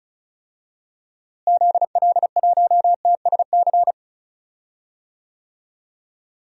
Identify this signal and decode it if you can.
Morse code, heard as a keyed tone, about 34 wpm, 700 Hz. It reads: ZL1TSC
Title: DXCC-100 Call Signs - 6 Characters - Repeated 3x - 35wpm